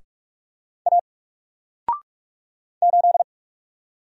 Morse code, heard as a keyed tone, about 46 wpm, 700 Hz. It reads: A 8
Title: Single Letter-Number 45wpm